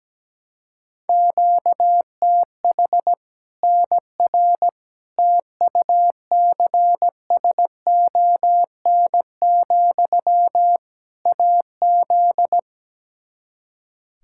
Morse code, heard as a keyed tone, about 17 wpm, 700 Hz. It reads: QTH NR TUCSON, AZ
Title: K1USN SST QSO Practice 17wpm